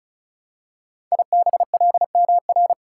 Morse code, heard as a keyed tone, about 36 wpm, 690 Hz. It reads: IBLMR
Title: Code Groups - 5 Letters - 35wpm